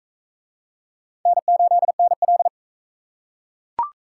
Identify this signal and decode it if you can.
Morse code, heard as a keyed tone, about 42 wpm, 700 Hz. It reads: N8NL